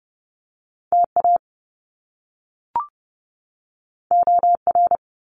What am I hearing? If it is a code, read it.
Morse code, heard as a keyed tone, about 30 wpm, 700 Hz. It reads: TA OR
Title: Top Two Letter Combinations in Words 30wpm